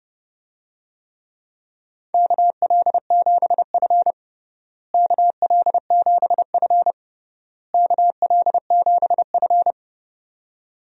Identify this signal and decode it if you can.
Morse code, heard as a keyed tone, about 30 wpm, 700 Hz. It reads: KL7F KL7F KL7F
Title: DXCC-100 Call Signs - 4 Characters - Repeated 3x - 30wpm